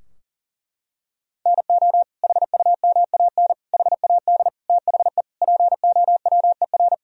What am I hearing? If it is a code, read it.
Morse code, heard as a keyed tone, about 40 wpm, 700 Hz. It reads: NO HUMAN HAD THE POWER
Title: Sentences from Top 1000 Words - Review Part 3 - 40wpm